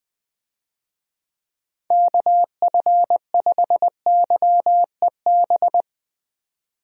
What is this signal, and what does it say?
Morse code, heard as a keyed tone, about 20 wpm, 700 Hz. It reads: KF5YEB
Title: USA Call Signs - 6 Characters - 20wpm